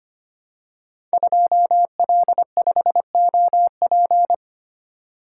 Morse code, heard as a keyed tone, about 25 wpm, 700 Hz. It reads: 2L5OP
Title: Code Groups - 5 Letter:Numbers - 25wpm